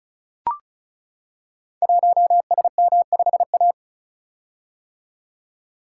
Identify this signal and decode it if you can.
Morse code, heard as a keyed tone, about 36 wpm, 700 Hz. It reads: 1SM5A
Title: Code Groups - 5 Letter:Numbers - 35wpm